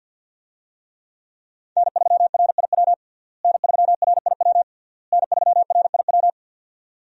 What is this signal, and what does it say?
Morse code, heard as a keyed tone, about 51 wpm, 700 Hz. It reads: N3RIW N3RIW N3RIW
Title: USA Call Signs - 5 Characters - Repeated 3x - 50wpm